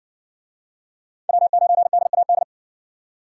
Morse code, heard as a keyed tone, about 53 wpm, 700 Hz. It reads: W9DAD